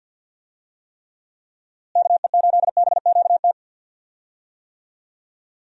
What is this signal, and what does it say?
Morse code, heard as a keyed tone, about 49 wpm, 700 Hz. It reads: KE8BQT
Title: USA Call Signs - 6 Characters - Repeated 2x - 50wpm